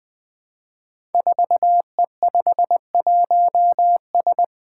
Morse code, heard as a keyed tone, about 20 wpm, 700 Hz. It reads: 4E51S
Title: Five Challenging Letter-Number Combinations 20wpm